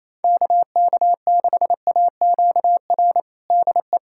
Morse code, heard as a keyed tone, about 28 wpm, 700 Hz. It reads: KK6AQR DE
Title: USA Callsign DE Callsign 28wpm